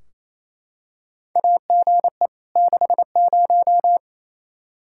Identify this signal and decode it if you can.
Morse code, heard as a keyed tone, about 28 wpm, 700 Hz. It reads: AGE 60
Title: Practice QSO - Exchange 3 - More Info - 28wpm